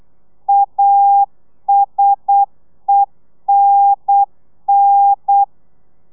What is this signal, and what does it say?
Morse code, heard as a keyed tone, about 8 wpm, 800 Hz. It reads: ASENN